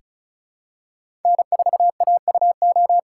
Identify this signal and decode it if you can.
Morse code, heard as a keyed tone, about 36 wpm, 700 Hz. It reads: N4AUO
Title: USA Call Signs - 5 Characters - 8x Character Spacing - 35wpm